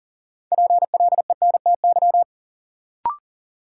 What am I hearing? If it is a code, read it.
Morse code, heard as a keyed tone, about 40 wpm, 700 Hz. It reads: PLENTY
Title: Sentences from Top 1300 Words - New Word Drill Alt - 40wpm